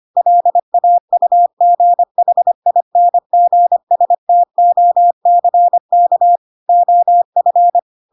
Morse code, heard as a keyed tone, about 25 wpm, 700 Hz. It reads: LAUGHINGSTOCK OF
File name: Pinocchio / Pinocchio ch4 25 25 ik1hge